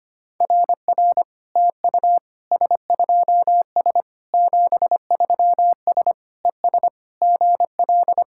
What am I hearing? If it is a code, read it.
Morse code, heard as a keyed tone, about 25 wpm, 690 Hz. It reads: RR TU S2S 73S ES GL
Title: SOTA QSO Practice 25wpm